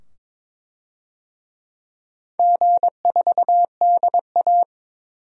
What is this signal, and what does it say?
Morse code, heard as a keyed tone, about 23 wpm, 700 Hz. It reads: G4DA
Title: DXCC-100 Call Signs - 4 Characters - 22wpm